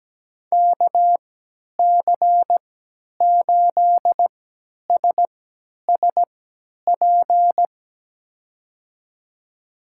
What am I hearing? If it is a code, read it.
Morse code, heard as a keyed tone, about 17 wpm, spaced as for 10 wpm, 700 Hz. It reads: KC8SSP
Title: USA Call Signs - 6 Characters - 3x Character Spacing - 17wpm